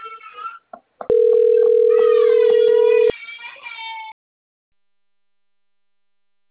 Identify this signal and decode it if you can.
Ringback tone (North America), 440 + 480 Hz.
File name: out- - - - - .2945